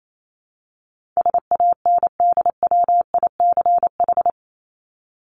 Morse code, heard as a keyed tone, about 28 wpm, 700 Hz. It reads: SANDWICH
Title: Top 1751-2000 Words 28wpm